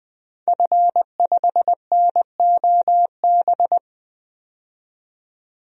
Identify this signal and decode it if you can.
Morse code, heard as a keyed tone, about 20 wpm, 700 Hz. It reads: F5NOB
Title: DXCC-100 Call Signs - 5 Characters - 20wpm